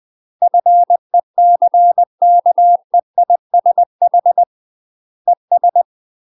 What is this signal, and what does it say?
Morse code, heard as a keyed tone, about 20 wpm, 700 Hz. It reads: FECKEISH ES
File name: UK MASTER QSOs TRAINING / 20WPM / UK MASTER QSOs 0018